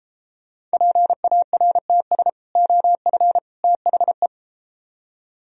Morse code, heard as a keyed tone, about 30 wpm, 700 Hz. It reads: PARTS OF THE